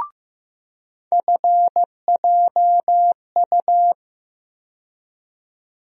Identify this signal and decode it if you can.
Morse code, heard as a keyed tone, about 15 wpm, 700 Hz. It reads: FJU
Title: USA Call Sign Postfixes - 3 Letters 15wpm